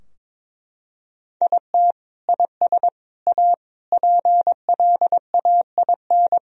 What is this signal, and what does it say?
Morse code, heard as a keyed tone, about 22 wpm, 700 Hz. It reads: IT IS A PLAIN
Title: Sentences from Top 1000 Words - Review Part 2 - 22wpm